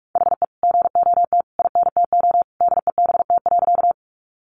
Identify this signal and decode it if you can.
Morse code, heard as a keyed tone, about 46 wpm, 700 Hz. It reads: HE GOT INTO DEBT.
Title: Sentences from Top 1300 Words 45wpm